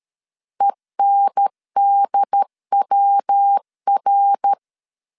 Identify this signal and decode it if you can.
Morse code, heard as a keyed tone, about 13 wpm, 800 Hz. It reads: ENDWR
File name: CW / DET / 1 3 1500C S C 12wpm 2750